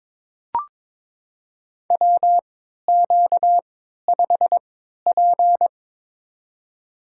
Morse code, heard as a keyed tone, about 21 wpm, spaced as for 12 wpm, 690 Hz. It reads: WQ5P
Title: USA Call Signs - 4 Characters - 3x Character Spacing - 22wpm